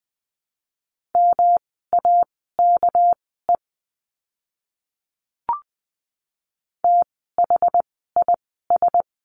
Morse code, heard as a keyed tone, about 20 wpm, spaced as for 14 wpm, 700 Hz. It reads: MAKE THIS
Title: 244 Lesson 39f - Letter z - Words 20wpm